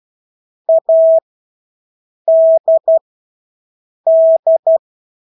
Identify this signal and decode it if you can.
Morse code, heard as a keyed tone, about 12 wpm, spaced as for 6 wpm, 650 Hz. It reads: ADD
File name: Engelske ord 12-6 wpm